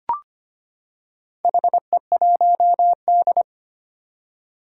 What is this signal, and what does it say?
Morse code, heard as a keyed tone, about 25 wpm, 700 Hz. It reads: HE1D
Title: Four Challenging Letter-Number Combinations 25wpm